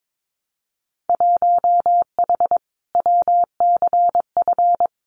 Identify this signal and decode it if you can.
Morse code, heard as a keyed tone, about 22 wpm, 700 Hz. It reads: 1H WCF
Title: Winter Field Day Exchanges Encore 22wpm